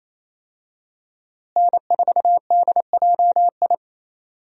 Morse code, heard as a keyed tone, about 28 wpm, 700 Hz. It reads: N4DJI
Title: USA Call Signs - 5 Characters - Repeated 3x - 28wpm